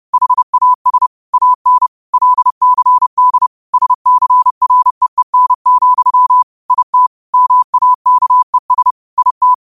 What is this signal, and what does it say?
Morse code, heard as a keyed tone, about 30 wpm, 1000 Hz. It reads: HAS AN LCD SCREEN, IT MAKES IT
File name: qotdfiles / 30wpm